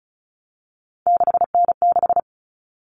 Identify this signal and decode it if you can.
Morse code, heard as a keyed tone, about 34 wpm, 700 Hz. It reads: 6N6
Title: All Cut Numbers - 3-Digit 35wpm